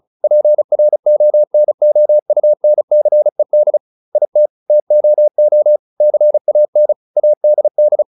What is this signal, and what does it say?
Morse code, heard as a keyed tone, about 34 wpm, 600 Hz. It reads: PRONOUNCED IT TOO CAN ADD